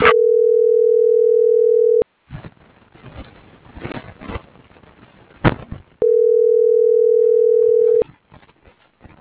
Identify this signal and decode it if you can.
Ringback tone (North America), 440 + 480 Hz.
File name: CENTRAL MAYAB